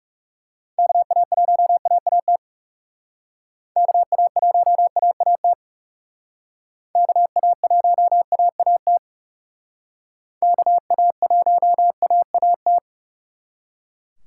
Morse code, heard as a keyed tone, about 36 wpm, 700 Hz. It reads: KA1AAT KA1AAT KA1AAT KA1AAT
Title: Speed Racing - USA 2x3 Call Signs 30wpm